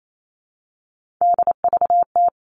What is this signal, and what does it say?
Morse code, heard as a keyed tone, about 28 wpm, 700 Hz. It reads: DVT
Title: All Cut Numbers - 3-Digit 28wpm